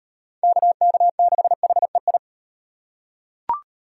Morse code, heard as a keyed tone, about 37 wpm, 700 Hz. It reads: KK6HEI